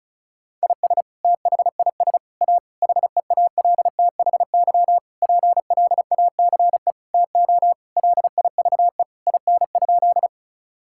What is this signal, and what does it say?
Morse code, heard as a keyed tone, about 36 wpm, 700 Hz. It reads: IS THIS A HEALTHY PLACE TO LIVE IN?
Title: Sentences from Top 1000 Words - Review Part 5 - 35wpm